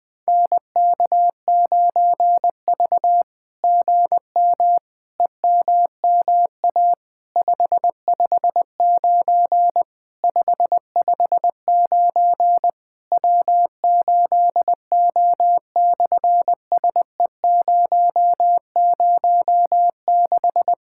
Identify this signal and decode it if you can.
Morse code, heard as a keyed tone, about 20 wpm, 700 Hz. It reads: NK9V GM EMMA 559 559 W8O/SE006
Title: SOTA QSO Practice 20wpm